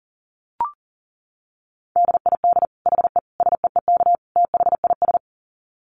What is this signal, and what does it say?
Morse code, heard as a keyed tone, about 40 wpm, 700 Hz. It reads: DID HE SEEK THIS
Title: Dit Challenge 40wpm